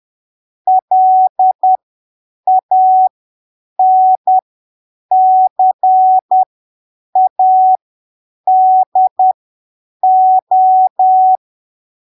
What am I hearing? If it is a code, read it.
Morse code, heard as a keyed tone, about 10 wpm, spaced as for 7 wpm, 750 Hz. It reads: LANCADO
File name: arquivos / texto / texto 16 10ppm